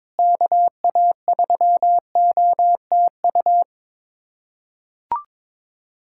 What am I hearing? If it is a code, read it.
Morse code, heard as a keyed tone, about 22 wpm, 700 Hz. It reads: KA3OTU
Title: USA Callsign DE Callsign 22wpm